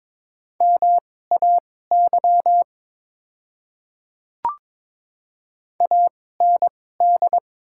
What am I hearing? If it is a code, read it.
Morse code, heard as a keyed tone, about 22 wpm, spaced as for 16 wpm, 700 Hz. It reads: MAY AND
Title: 2x Character Spacing - Top 100 Words - 22wpm